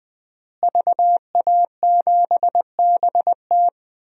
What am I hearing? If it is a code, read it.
Morse code, heard as a keyed tone, about 21 wpm, 700 Hz. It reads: VA7BT
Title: DXCC-100 Call Signs - 5 Characters - Repeated 2x - 20wpm